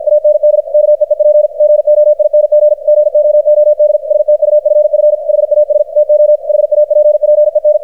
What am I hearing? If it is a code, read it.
Morse code, heard as a keyed tone, about 52 wpm, 600 Hz, with radio noise. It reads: AND KEEP GOING GOOD STUFF HAS TO HAPPEN
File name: archived / CRSnet